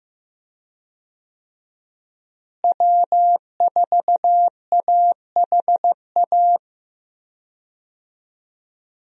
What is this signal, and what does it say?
Morse code, heard as a keyed tone, about 15 wpm, 700 Hz. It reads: W4AHA
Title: USA Call Signs - 5 Characters - Repeated 3x - 15wpm